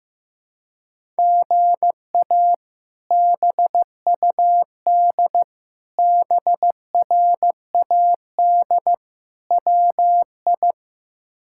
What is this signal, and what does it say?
Morse code, heard as a keyed tone, about 15 wpm, 700 Hz. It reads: GA BUD BRAD WI
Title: K1USN SST QSO Practice - 15wpm